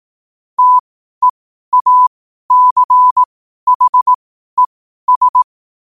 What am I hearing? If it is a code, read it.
Morse code, heard as a keyed tone, about 18 wpm, spaced as for 12 wpm, 1000 Hz. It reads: TEACHES
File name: qotdfiles / 12wpm